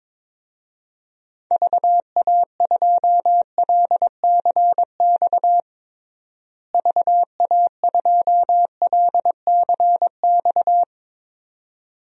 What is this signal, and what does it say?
Morse code, heard as a keyed tone, about 22 wpm, 700 Hz. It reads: VA2LCX VA2LCX
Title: DXCC-100 Call Signs - 6 Characters - Repeated 2x - 22wpm